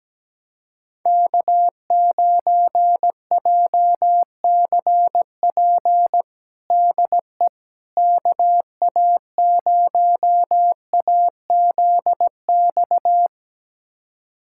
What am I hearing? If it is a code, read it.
Morse code, heard as a keyed tone, about 17 wpm, 700 Hz. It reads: K9JCP DE KA0AZX
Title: USA Callsign DE Callsign 17wpm